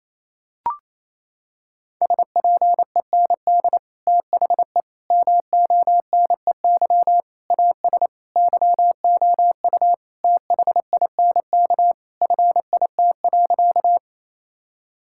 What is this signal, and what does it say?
Morse code, heard as a keyed tone, about 28 wpm, 700 Hz. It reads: SPEND THE MONEY AS YOU THINK FIT.
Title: Sentences from Top 1000 Words - Review Part 5 - 28wpm